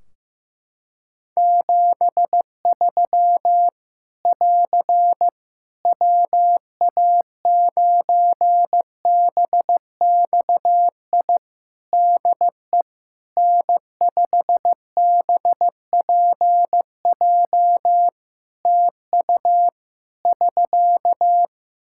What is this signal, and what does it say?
Morse code, heard as a keyed tone, about 16 wpm, 700 Hz. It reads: 73 <AR> WA9BXI DE N5BPJ TU <SK>
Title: K1USN SST QSO Practice 15wpm